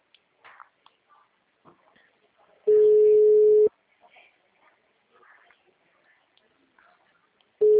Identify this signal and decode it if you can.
Ringback tone (Europe, 425 Hz), with speech.